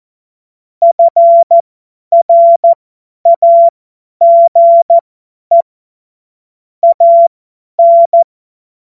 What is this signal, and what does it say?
Morse code, heard as a keyed tone, about 14 wpm, spaced as for 10 wpm, 680 Hz. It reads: FRAGE AN
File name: Morsekurs / Witze / Witz 06. Januar 2026 50 F70